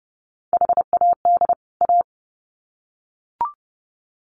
Morse code, heard as a keyed tone, about 30 wpm, 700 Hz. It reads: HAD A
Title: Sets of 2 Words 30wpm